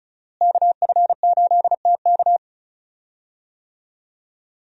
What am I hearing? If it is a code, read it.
Morse code, heard as a keyed tone, about 34 wpm, 700 Hz. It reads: KF8TK
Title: USA Call Signs - 5 Characters - Repeated 3x - 35wpm